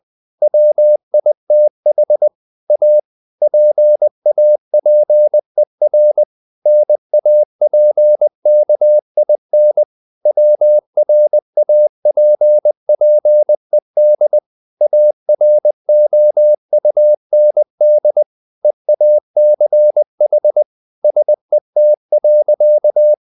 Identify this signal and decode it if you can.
Morse code, heard as a keyed tone, about 20 wpm, 600 Hz. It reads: WITH A PAPER NAPKIN WRAPPED AROUND EACH SET.
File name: the old man and the sea / the old man and the sea part 1 20wpm